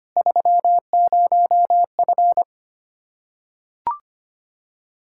Morse code, heard as a keyed tone, about 25 wpm, 700 Hz. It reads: 30F
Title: Practice QSO - Exchange 2 - Info - 25wpm